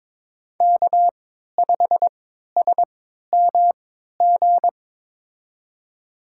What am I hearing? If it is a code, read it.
Morse code, heard as a keyed tone, about 23 wpm, spaced as for 13 wpm, 700 Hz. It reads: K5SMG
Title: USA Call Signs - 3x Character Spacing - 22wpm